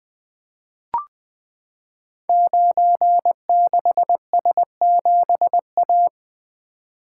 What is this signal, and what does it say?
Morse code, heard as a keyed tone, about 20 wpm, 700 Hz. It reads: 96S7A
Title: Code Groups - 5 Letter:Numbers - 20wpm